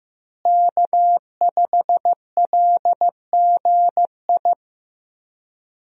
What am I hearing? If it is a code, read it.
Morse code, heard as a keyed tone, about 15 wpm, 700 Hz. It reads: K5LGI
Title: USA Call Signs - 5 Characters - Repeated 3x - 15wpm